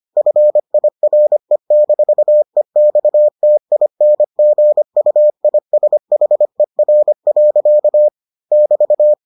Morse code, heard as a keyed tone, about 25 wpm, 600 Hz. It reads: FIRE-EXTINGUISHER. <BT>